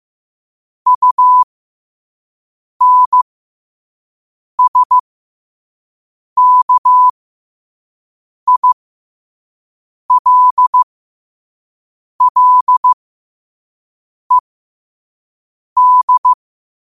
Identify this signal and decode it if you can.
Morse code, heard as a keyed tone, about 15 wpm, spaced as for 5 wpm, 1000 Hz. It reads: UNSKILLED
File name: qotdfiles / 5wpm